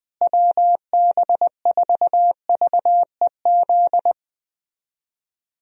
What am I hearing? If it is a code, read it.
Morse code, heard as a keyed tone, about 20 wpm, 700 Hz. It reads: WB4VEZ
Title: USA Call Signs - 6 Characters - Repeated 2x - 20wpm